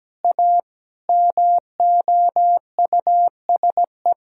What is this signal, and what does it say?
Morse code, heard as a keyed tone, about 17 wpm, 700 Hz. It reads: A MOUSE
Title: Sentences from Top 1000 Words - Review Part 5 - 17wpm